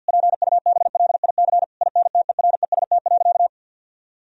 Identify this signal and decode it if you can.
Morse code, heard as a keyed tone, about 50 wpm, 700 Hz. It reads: PUBLIC INTEREST.
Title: Sentences from Top 1000 Words - Review Part 4 - 50wpm